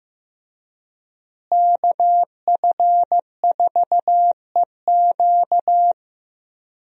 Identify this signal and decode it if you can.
Morse code, heard as a keyed tone, about 14 wpm, 700 Hz. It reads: KF4EQ
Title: USA Call Signs - 5 Characters - Repeated 2x - 15wpm